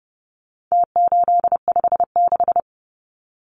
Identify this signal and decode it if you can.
Morse code, heard as a keyed tone, about 30 wpm, 700 Hz. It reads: T856
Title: Numbers - Contest Style - 4-Digit 30wpm